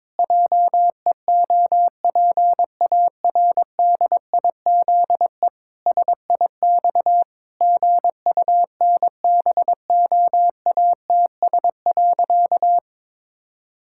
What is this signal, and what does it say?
Morse code, heard as a keyed tone, about 22 wpm, 700 Hz. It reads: JEOPARDIZE SIX GUNBOATS.
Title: Pangram Challenge v2 - 22wpm